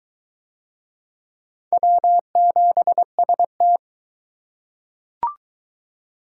Morse code, heard as a keyed tone, about 22 wpm, 700 Hz. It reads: W7ST